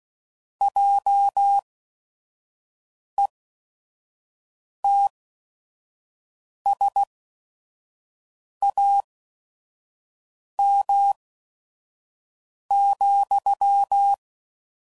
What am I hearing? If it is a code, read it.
Morse code, heard as a keyed tone, about 16 wpm, spaced as for 5 wpm, 800 Hz. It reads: JETSAM,